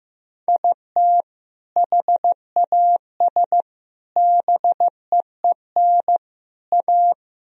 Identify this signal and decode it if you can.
Morse code, heard as a keyed tone, about 15 wpm, 700 Hz. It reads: IT HAS BEEN A
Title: Sentences from Top 1000 Words - Review Part 1 - 15wpm